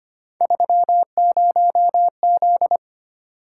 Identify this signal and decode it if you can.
Morse code, heard as a keyed tone, about 25 wpm, 700 Hz. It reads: 30Z
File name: Three Letter-Number Combinations - 25wpm